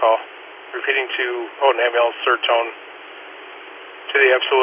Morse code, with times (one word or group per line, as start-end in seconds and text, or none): none